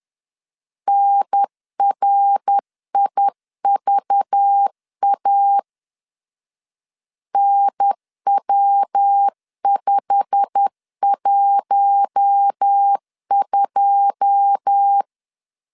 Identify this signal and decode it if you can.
Morse code, heard as a keyed tone, about 10 wpm, 800 Hz. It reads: NRIVA NW512
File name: LEZIONI / 9 2 10wpm 2736